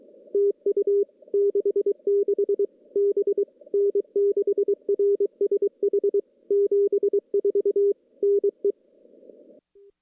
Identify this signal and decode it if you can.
Morse code, heard as a keyed tone, about 23 wpm, 400 Hz. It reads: TU 66 B N6RSH 74 NE
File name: CW / CW-SS-k5tr- -1700